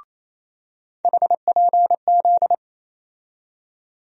Morse code, heard as a keyed tone, about 28 wpm, 700 Hz. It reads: HPZ
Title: USA Call Sign Postfixes - 3 Letters 28wpm